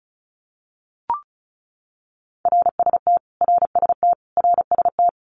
Morse code, heard as a keyed tone, about 34 wpm, 700 Hz. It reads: RST RST RST
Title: Abbreviations Repeated 3x - 35wpm